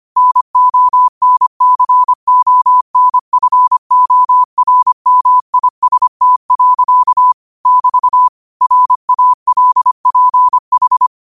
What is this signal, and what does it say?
Morse code, heard as a keyed tone, about 25 wpm, 1000 Hz. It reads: NONCONFORMIST. <BT> RALPH
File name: 25wpm